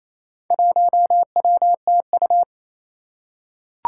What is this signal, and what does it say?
Morse code, heard as a keyed tone, about 29 wpm, 700 Hz. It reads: 1WTU
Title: Four Challenging Letter-Number Combinations 28wpm